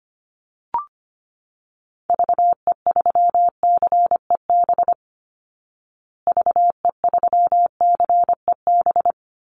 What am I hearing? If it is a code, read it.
Morse code, heard as a keyed tone, about 25 wpm, 700 Hz. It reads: VE3CEB VE3CEB
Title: Call Signs Repeated 2x - 25wpm